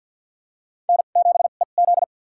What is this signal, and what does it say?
Morse code, heard as a keyed tone, about 53 wpm, spaced as for 37 wpm, 700 Hz. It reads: N7EZ
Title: USA Call Signs - 4 Characters - 2x Character Spacing - 50wpm